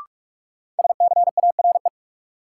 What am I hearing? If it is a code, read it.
Morse code, heard as a keyed tone, about 46 wpm, 700 Hz. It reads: SCARE
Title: Sentences from Top 1000 Words - Review Part 5 - New Word Drill - 45wpm